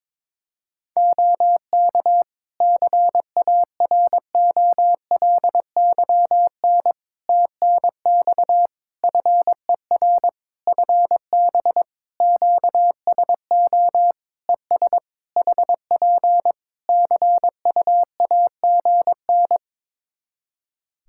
Morse code, heard as a keyed tone, about 22 wpm, 700 Hz. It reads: OK CAROLYN TNX FER FB QSO ES HP CUAGN
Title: Practice QSO - Exchange 4 - Ending - 22wpm